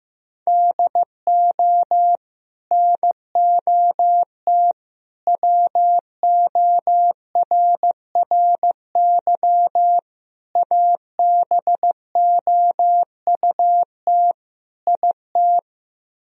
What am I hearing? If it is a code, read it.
Morse code, heard as a keyed tone, about 15 wpm, 700 Hz. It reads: DO NOT WORRY ABOUT IT
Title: Speed-Racing - Sets of 5 Words 15wpm